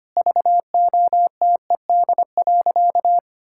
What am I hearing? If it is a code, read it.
Morse code, heard as a keyed tone, about 25 wpm, 700 Hz. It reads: VOTED.
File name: practice / 25wpm / Sentences with 3-Word Phrases - 25wpm